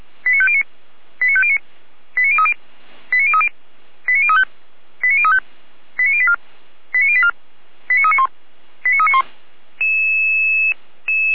Five-tone selective call (ZVEI), heard as quick runs of five tones, tones of about 70 ms. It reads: 89509 89509 89030 89030 89035 89035 89084 89084 89391 89391
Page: /455/SDR/ZVEI/